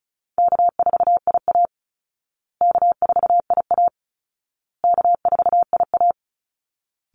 Morse code, heard as a keyed tone, about 34 wpm, 700 Hz. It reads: K4IA K4IA K4IA
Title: Easy Call Signs Repeated 3x - 35wpm